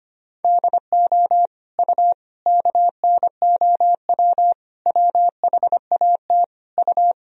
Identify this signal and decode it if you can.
Morse code, heard as a keyed tone, about 25 wpm, 700 Hz. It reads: DO U KNOW WHAT U
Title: Sentences from Top 100 Words and Abbreviations 25wpm